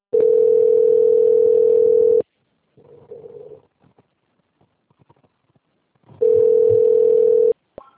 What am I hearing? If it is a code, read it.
Ringback tone (North America), 440 + 480 Hz.
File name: out- - - - - .757